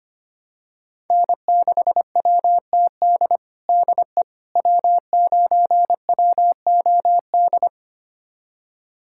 Morse code, heard as a keyed tone, about 25 wpm, 700 Hz. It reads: N6WTD DE W9WOD
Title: USA Callsign DE Callsign - 5 Characters - 25wpm